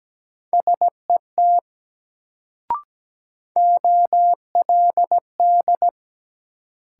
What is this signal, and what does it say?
Morse code, heard as a keyed tone, about 17 wpm, 700 Hz. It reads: SET OLD
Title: Sentences from Top 200 Words - New Word Drill - 17wpm